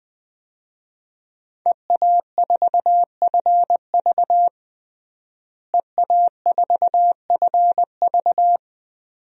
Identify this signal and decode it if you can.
Morse code, heard as a keyed tone, about 20 wpm, 700 Hz. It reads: EA4FV EA4FV
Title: DXCC-100 Call Signs - 5 Characters - Repeated 2x - 20wpm